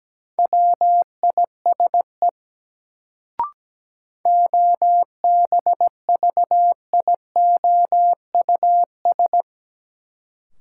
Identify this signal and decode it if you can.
Morse code, heard as a keyed tone, about 17 wpm, 700 Hz. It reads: WISE OBVIOUS
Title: Sentences from Top 1600 Words - New Word Drill - 17wpm